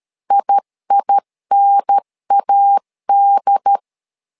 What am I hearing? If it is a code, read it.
Morse code, heard as a keyed tone, about 13 wpm, 800 Hz. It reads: IINAD
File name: LEZIONI / 2 UHDN / 2 UHDN 12wpm 1456